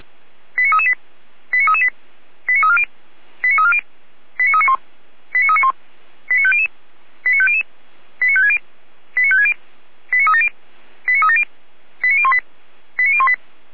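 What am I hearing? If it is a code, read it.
Five-tone selective call (ZVEI), heard as quick runs of five tones, tones of about 70 ms. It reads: 89338 89338 89340 89340 89391 89391 89500 89500 89560 89560 89370 89370 89018 89018
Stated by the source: ZVEI five-tone call received on an SDR.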